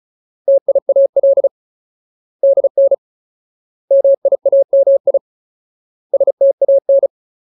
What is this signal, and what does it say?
Morse code, heard as a keyed tone, about 34 wpm, 550 Hz. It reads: TIAL DN MIAMI STAN
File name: Ultimate Word List / 35WPM Ultimate